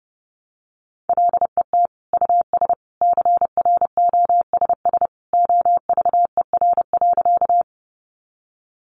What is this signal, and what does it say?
Morse code, heard as a keyed tone, about 30 wpm, 700 Hz. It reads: LET US CROSS OVER.
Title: Sentences from Top 300 Words 30wpm